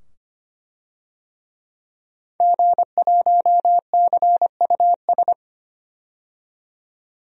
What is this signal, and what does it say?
Morse code, heard as a keyed tone, about 25 wpm, 700 Hz. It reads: G1CUS
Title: DXCC-100 Call Signs - 5 Characters - 25wpm